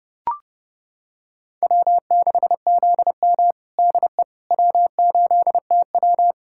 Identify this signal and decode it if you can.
Morse code, heard as a keyed tone, about 30 wpm, 700 Hz. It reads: W6ZM DE W8TW
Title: USA Callsign DE Callsign - 4 Characters - 30wpm